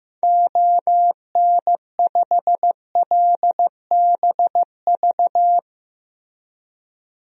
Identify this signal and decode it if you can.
Morse code, heard as a keyed tone, about 16 wpm, 700 Hz. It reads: ON5LBV
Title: DXCC-100 Call Signs - 6 Characters - Repeated 2x - 15wpm